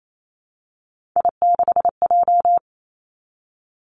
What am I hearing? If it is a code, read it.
Morse code, heard as a keyed tone, about 28 wpm, 700 Hz. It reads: I6J
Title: Three Challenging Letter-Number Combinations 28wpm